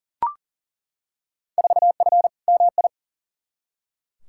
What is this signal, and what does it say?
Morse code, heard as a keyed tone, about 40 wpm, 700 Hz. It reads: 4F MI
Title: Field Day Exchanges Encore - 40wpm